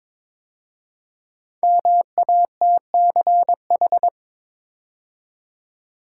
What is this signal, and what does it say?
Morse code, heard as a keyed tone, about 22 wpm, 700 Hz. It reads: MATCH
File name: 5 Letter Words - First 3 Letters Given - 22wpm